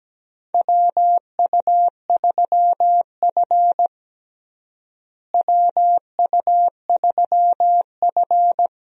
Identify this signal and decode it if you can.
Morse code, heard as a keyed tone, about 17 wpm, 700 Hz. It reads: WU3F WU3F
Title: USA Call Signs - 4 Characters - Repeated 3x - 17wpm